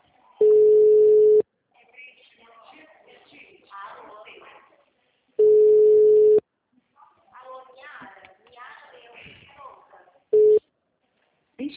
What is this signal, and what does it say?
Ringback tone (Europe, 425 Hz), with speech.